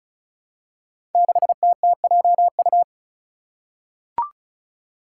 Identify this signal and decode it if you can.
Morse code, heard as a keyed tone, about 34 wpm, 700 Hz. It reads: 6TTJU
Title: Five Challenging Letter-Number Combinations 35wpm